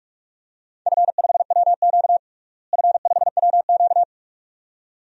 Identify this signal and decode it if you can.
Morse code, heard as a keyed tone, about 42 wpm, 700 Hz. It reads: F5WQ F5WQ
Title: DXCC-100 Call Signs - 4 Characters - Repeated 2x - 45wpm